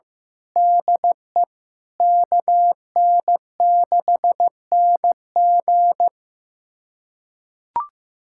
Morse code, heard as a keyed tone, about 15 wpm, 700 Hz. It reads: DE KN6NG
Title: USA Callsign DE Callsign - 5 Characters - 15wpm